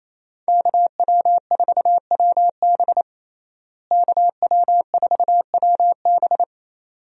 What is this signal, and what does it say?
Morse code, heard as a keyed tone, about 28 wpm, 700 Hz. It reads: KW4WB KW4WB
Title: USA Call Signs - 5 Characters - Repeated 2x - 28wpm